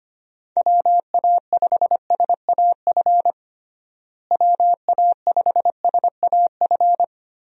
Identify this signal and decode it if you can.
Morse code, heard as a keyed tone, about 25 wpm, 700 Hz. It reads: WA5SAF WA5SAF
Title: USA Call Signs - 6 Characters - Repeated 2x - 25wpm